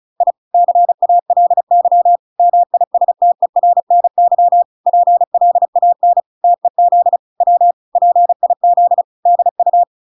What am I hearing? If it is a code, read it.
Morse code, heard as a keyed tone, about 34 wpm, 700 Hz. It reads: I CALY MISTERNY PLAN TEZ W PIZDU